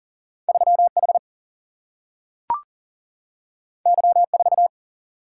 Morse code, heard as a keyed tone, about 41 wpm, 700 Hz. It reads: 3H Y4
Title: Single-Letter Single-Number 40wpm